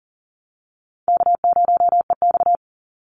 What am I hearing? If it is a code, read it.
Morse code, heard as a keyed tone, about 40 wpm, 700 Hz. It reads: K0EX
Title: Easy Call Signs - 40wpm